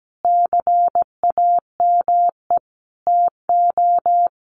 Morse code, heard as a keyed tone, about 17 wpm, 700 Hz. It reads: CAME TO
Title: Sentences from Top 1000 Words 17wpm